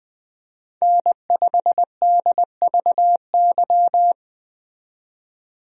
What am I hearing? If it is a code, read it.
Morse code, heard as a keyed tone, about 20 wpm, 700 Hz. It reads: N5DVY
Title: USA Call Signs - 5 Characters - 20wpm